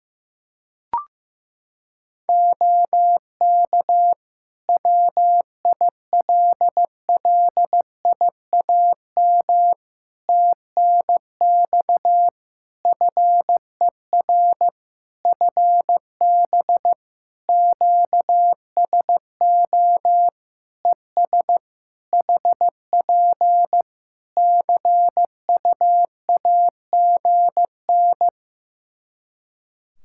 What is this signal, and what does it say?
Morse code, heard as a keyed tone, about 15 wpm, 700 Hz. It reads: OK WILLIAM TNX FER FB QSO ES HP CUAGN
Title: Practice QSO - Exchange 4 - Ending - 15wpm